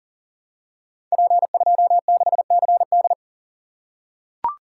Morse code, heard as a keyed tone, about 42 wpm, 690 Hz. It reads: P26CD
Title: Code Groups - 5 Letter:Numbers - 40wpm